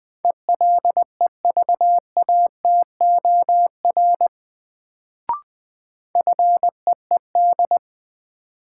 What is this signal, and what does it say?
Morse code, heard as a keyed tone, about 20 wpm, 700 Hz. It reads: ELEVATOR FEED
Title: Sentences from Top 1600 Words - New Word Drill - 20wpm